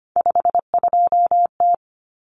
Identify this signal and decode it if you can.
Morse code, heard as a keyed tone, about 25 wpm, 700 Hz. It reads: 52T
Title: Common Cut Numbers - 3-Digit 25wpm